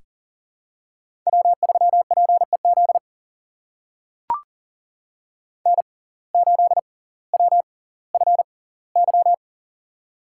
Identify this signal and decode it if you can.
Morse code, heard as a keyed tone, about 40 wpm, spaced as for 20 wpm, 700 Hz. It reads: W3PEZ N8WFY
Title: USA Call Signs - 5 Characters - 6x Character Spacing - 40wpm